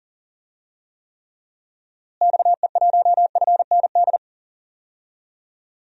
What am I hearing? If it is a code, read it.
Morse code, heard as a keyed tone, about 40 wpm, 700 Hz. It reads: XE1FND
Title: DXCC-100 Call Signs - 6 Characters - Repeated 3x - 40wpm